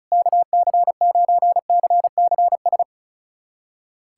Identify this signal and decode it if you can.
Morse code, heard as a keyed tone, about 36 wpm, 700 Hz. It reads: KC9CCS
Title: USA Callsign DE Callsign 35wpm